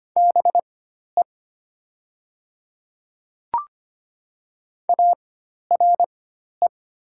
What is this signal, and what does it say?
Morse code, heard as a keyed tone, about 25 wpm, spaced as for 12 wpm, 700 Hz. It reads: BE ARE
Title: 4x Character Spacing - Top 500 Words - 25wpm